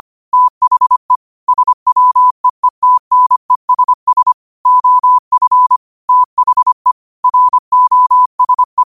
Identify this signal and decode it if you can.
Morse code, heard as a keyed tone, about 25 wpm, 1000 Hz. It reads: THE SWEETNESS OF THE ROSE